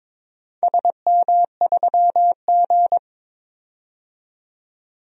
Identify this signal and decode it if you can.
Morse code, heard as a keyed tone, about 22 wpm, 700 Hz. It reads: SM3G
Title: DXCC-100 Call Signs - 4 Characters - 22wpm